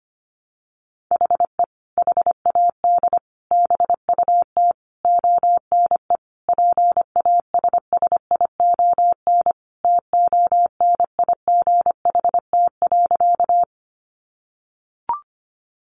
Morse code, heard as a keyed tone, about 25 wpm, 700 Hz. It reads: HE HAD BUT ONE PASSION TONIGHT.
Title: Sentences from Top 1300 Words 25wpm